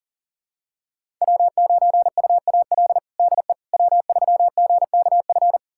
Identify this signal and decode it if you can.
Morse code, heard as a keyed tone, about 40 wpm, 700 Hz. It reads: W9UAL DE W3GKF
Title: USA Callsign DE Callsign - 5 Characters - 40wpm